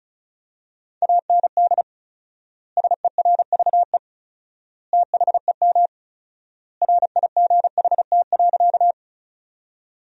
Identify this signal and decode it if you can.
Morse code, heard as a keyed tone, about 36 wpm, 700 Hz. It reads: AND SERVE THEM RIGHT.
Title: 4x Word-Spacing - Sentences from Top 400 Words - 35wpm